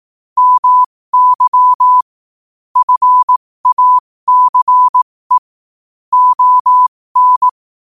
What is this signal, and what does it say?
Morse code, heard as a keyed tone, about 18 wpm, spaced as for 15 wpm, 1000 Hz. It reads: MY FACE ON